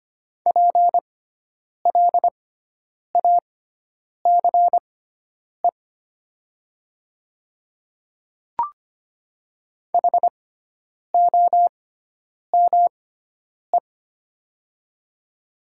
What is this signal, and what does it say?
Morse code, heard as a keyed tone, about 25 wpm, spaced as for 9 wpm, 700 Hz. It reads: PLACE HOME
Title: 6x Character Spacing - Top 200 Words - 25wpm